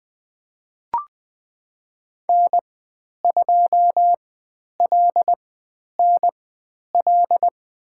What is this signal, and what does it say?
Morse code, heard as a keyed tone, about 20 wpm, spaced as for 10 wpm, 700 Hz. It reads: N2LNL
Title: 102 Lesson 17g - Letter u - Call Signs 20wpm